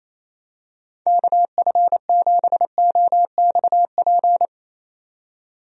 Morse code, heard as a keyed tone, about 28 wpm, 700 Hz. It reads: KF7OXP
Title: USA Call Signs - 6 Characters - 28wpm